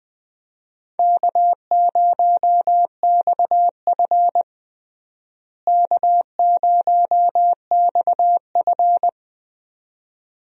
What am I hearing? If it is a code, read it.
Morse code, heard as a keyed tone, about 20 wpm, 700 Hz. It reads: K0XF K0XF
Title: USA Call Signs - 4 Characters - Repeated 2x - 20wpm